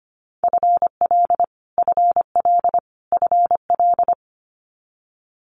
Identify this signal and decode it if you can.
Morse code, heard as a keyed tone, about 25 wpm, 700 Hz. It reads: FL FL FL
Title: Multi-letter Phonemes Repeated 3x 25wpm